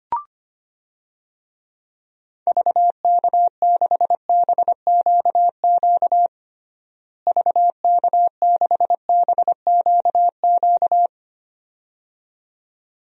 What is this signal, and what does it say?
Morse code, heard as a keyed tone, about 25 wpm, 700 Hz. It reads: VK6BQQ VK6BQQ
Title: DXCC-100 Call Signs - 6 Characters - Repeated 2x - 25wpm